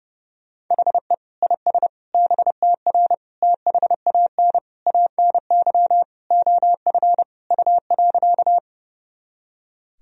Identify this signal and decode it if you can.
Morse code, heard as a keyed tone, about 30 wpm, 700 Hz. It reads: HE IS BTR THAN ANY OF U.
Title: Sentences from Top 400 Words and Abbreviations 30wpm